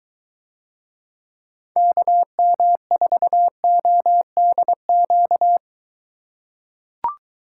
Morse code, heard as a keyed tone, about 21 wpm, 700 Hz. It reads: KM4ODQ